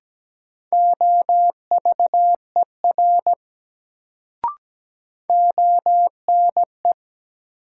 Morse code, heard as a keyed tone, about 17 wpm, 700 Hz. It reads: OVER ONE
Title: Sentences from Top 100 Words - New Word Drill - 17wpm